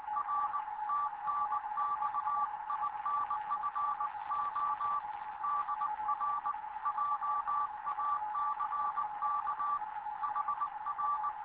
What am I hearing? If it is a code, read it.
Morse code, heard as a keyed tone, about 19 wpm, 1160 Hz, with radio noise. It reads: R TNX INFO DR JACK HR
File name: Morze-HA2MN / hangp / 2 11 GYAKORLAT